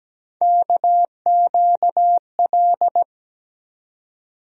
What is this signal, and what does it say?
Morse code, heard as a keyed tone, about 17 wpm, 700 Hz. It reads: KQL
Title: USA Call Sign Postfixes - 3 Letters 17wpm